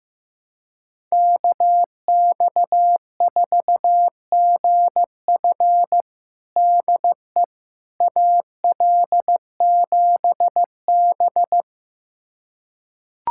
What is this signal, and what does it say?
Morse code, heard as a keyed tone, about 15 wpm, 700 Hz. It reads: KX4GF DE AL7B
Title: USA Callsign DE Callsign 15wpm